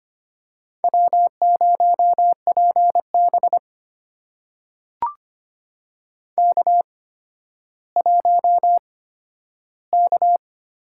Morse code, heard as a keyed tone, about 25 wpm, spaced as for 12 wpm, 700 Hz. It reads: W0PB K1K
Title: USA Call Signs - 4 Characters - 8x Character Spacing - 25wpm